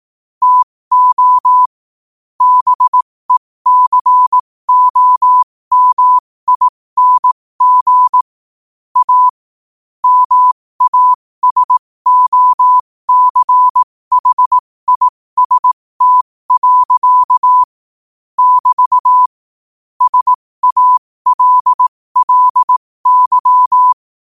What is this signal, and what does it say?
Morse code, heard as a keyed tone, about 18 wpm, spaced as for 15 wpm, 1000 Hz. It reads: TO BECOMING A MASOCHIST. <BT> SALLY
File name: qotdfiles / 15wpm